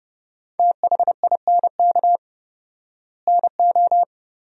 Morse code, heard as a keyed tone, about 30 wpm, 700 Hz. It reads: THINK NO
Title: 4x Word-Spacing - Sentences from Top 200 Words - 30wpm